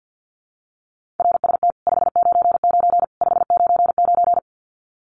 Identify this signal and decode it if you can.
Morse code, heard as a keyed tone, about 50 wpm, 700 Hz. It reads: RST 599 599
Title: Common QSO Phrases 50wpm